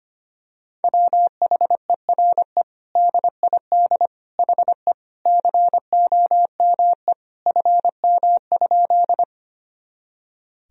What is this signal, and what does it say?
Morse code, heard as a keyed tone, about 25 wpm, 700 Hz. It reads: WHERE DID HE COME FM?
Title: Sentences from Top 200 Words and Abbreviations 25wpm